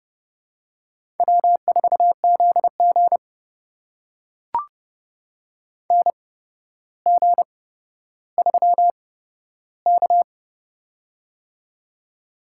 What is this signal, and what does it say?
Morse code, heard as a keyed tone, about 30 wpm, spaced as for 13 wpm, 700 Hz. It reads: W4ZG NG3K
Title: USA Call Signs - 4 Characters - 8x Character Spacing - 30wpm